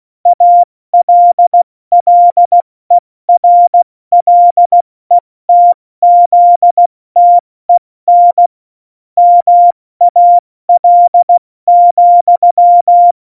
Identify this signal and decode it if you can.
Morse code, heard as a keyed tone, about 16 wpm, 700 Hz. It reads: ALLERLETZTEN MAL,
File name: Morsekurs / Witze / Witz 05. Januar 2021, 70 F80